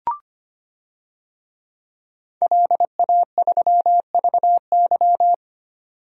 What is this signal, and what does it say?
Morse code, heard as a keyed tone, about 25 wpm, 700 Hz. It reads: LA3VY
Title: DXCC-100 Call Signs - 5 Characters - 25wpm